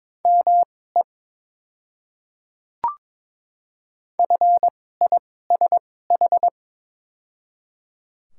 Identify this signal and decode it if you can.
Morse code, heard as a keyed tone, about 22 wpm, spaced as for 16 wpm, 700 Hz. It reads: ME FISH
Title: 2x Character Spacing - Top 300 Words - 22wpm